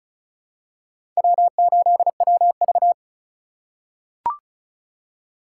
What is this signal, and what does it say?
Morse code, heard as a keyed tone, about 36 wpm, 700 Hz. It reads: W8WV
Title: USA Call Signs - 4 Characters - 3x Character Spacing - 35wpm